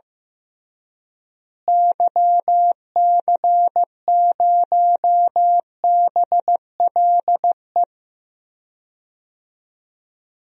Morse code, heard as a keyed tone, about 14 wpm, 700 Hz. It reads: YC0BLE
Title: DXCC-100 Call Signs - 6 Characters - Repeated 3x - 15wpm